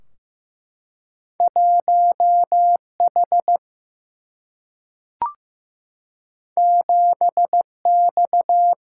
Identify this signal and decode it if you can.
Morse code, heard as a keyed tone, about 15 wpm, 700 Hz. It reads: 1 H 7 X
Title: Single-Letter Single-Number 15wpm